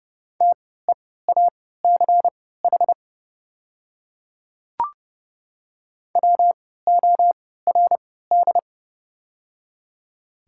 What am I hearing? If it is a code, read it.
Morse code, heard as a keyed tone, about 30 wpm, spaced as for 17 wpm, 700 Hz. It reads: TEACH WORD
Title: 3x Character Spacing - Top 500 Words - 30wpm